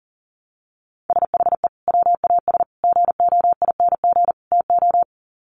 Morse code, heard as a keyed tone, about 40 wpm, 700 Hz. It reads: SHE WAS GOING TO
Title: Sets of 4 Words 40wpm